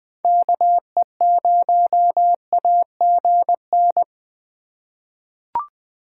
Morse code, heard as a keyed tone, about 20 wpm, 700 Hz. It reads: KE0AGN
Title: USA Call Signs - 6 Characters - 6x Character Spacing - 20wpm